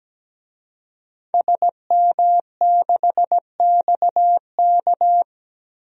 Morse code, heard as a keyed tone, about 17 wpm, 700 Hz. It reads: SM6XK
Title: DXCC-100 Call Signs - 5 Characters - 17wpm